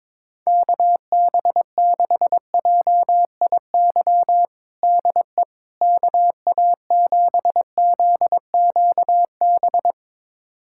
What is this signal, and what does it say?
Morse code, heard as a keyed tone, about 22 wpm, 700 Hz. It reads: KB6JIY DE KA7ZQB
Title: USA Callsign DE Callsign - 6 Characters - 22wpm